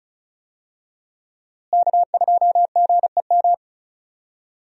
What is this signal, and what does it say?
Morse code, heard as a keyed tone, about 34 wpm, 700 Hz. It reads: K2GEM
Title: USA Call Signs - 5 Characters - Repeated 2x - 35wpm